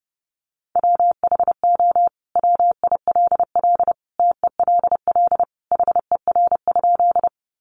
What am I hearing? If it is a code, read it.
Morse code, heard as a keyed tone, about 30 wpm, 700 Hz. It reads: WHO WILL TELL HER?
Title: Sentences from Top 200 Words 30wpm